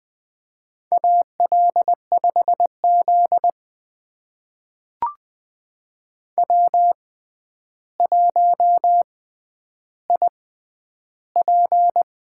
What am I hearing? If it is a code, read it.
Morse code, heard as a keyed tone, about 20 wpm, spaced as for 10 wpm, 700 Hz. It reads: AL5Z W1IP
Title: USA Call Signs - 4 Characters - 6x Character Spacing - 20wpm